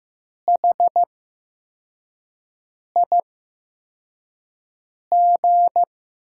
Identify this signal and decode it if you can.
Morse code, heard as a keyed tone, about 15 wpm, spaced as for 4 wpm, 700 Hz. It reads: HIG
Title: USA Call Signs - 8x Character Spacing - 15wpm